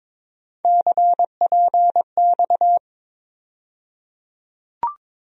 Morse code, heard as a keyed tone, about 21 wpm, 700 Hz. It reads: CPX
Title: USA Call Sign Postfixes - 3 Letters 22wpm